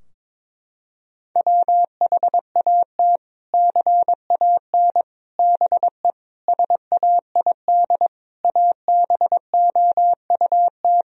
Morse code, heard as a keyed tone, about 22 wpm, 700 Hz. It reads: WHAT CAN BE SAID ABOUT
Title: Sentences from Top 1000 Words - Review Part 4 - 22wpm